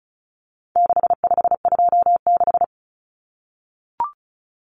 Morse code, heard as a keyed tone, about 34 wpm, 700 Hz. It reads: 6526
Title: Numbers - 4-Digit 35wpm